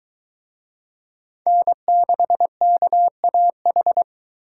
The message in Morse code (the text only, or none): N6KAH